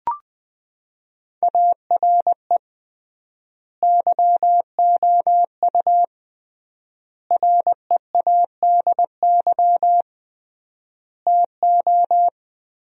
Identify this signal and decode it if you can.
Morse code, heard as a keyed tone, about 20 wpm, 700 Hz. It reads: ARE YOU READY TO
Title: 3x Word-Spacing - Sentences from Top 400 Words - 20wpm